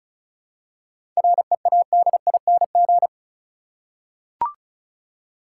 Morse code, heard as a keyed tone, about 36 wpm, 700 Hz. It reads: READING
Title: Seven Letter Words - 35wpm